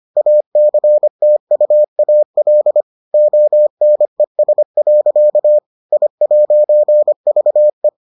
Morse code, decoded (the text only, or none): ACTUAL ONES. I'VE